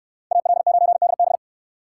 Morse code, heard as a keyed tone, about 62 wpm, 700 Hz. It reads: AL9DB